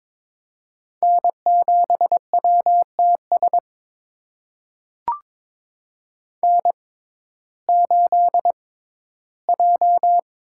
Morse code, heard as a keyed tone, about 22 wpm, spaced as for 14 wpm, 700 Hz. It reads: N7WTS N8J
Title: USA Call Signs - 5 Characters - 6x Character Spacing - 22wpm